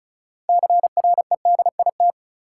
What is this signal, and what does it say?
Morse code, heard as a keyed tone, about 34 wpm, 700 Hz. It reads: CREDIT
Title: Sentences from Top 1000 Words - New Word Drill - 35wpm